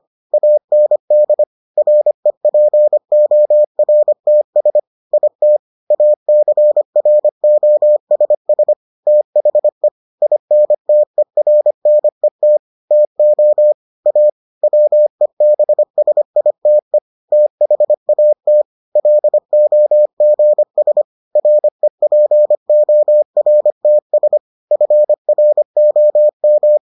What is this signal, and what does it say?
Morse code, heard as a keyed tone, about 25 wpm, 600 Hz. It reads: AND REPORTS IT ACROSS THE INTERNET TO A WEBSITE THAT LOGS REPORTS FROM